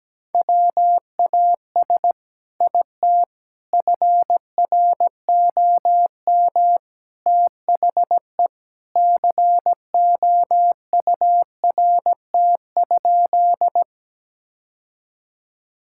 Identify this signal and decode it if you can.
Morse code, heard as a keyed tone, about 17 wpm, 700 Hz. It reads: WAS IT FROM THE COURT?
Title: Sentences from Top 1600 Words 17wpm